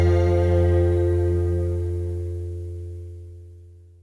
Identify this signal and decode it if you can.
Dial tone (North America), 350 + 440 Hz.